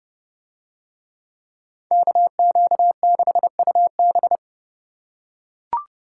Morse code, heard as a keyed tone, about 31 wpm, 700 Hz. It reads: KQ6UB